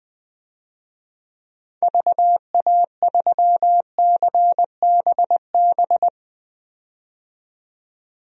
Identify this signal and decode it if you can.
Morse code, heard as a keyed tone, about 20 wpm, 700 Hz. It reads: VA3CBB
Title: DXCC-100 Call Signs - 6 Characters - 20wpm